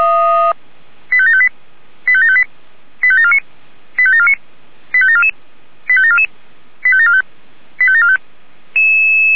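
Five-tone selective call (ZVEI), heard as quick runs of five tones, tones of about 70 ms. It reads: 85758 85758 85749 85749 85744 85744 85745 85745
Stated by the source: ZVEI five-tone selective call 85726 received on an SDR.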